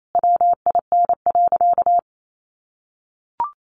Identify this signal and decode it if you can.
Morse code, heard as a keyed tone, about 28 wpm, 700 Hz. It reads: WIN.
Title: Sentences from Top 1500 Words 28wpm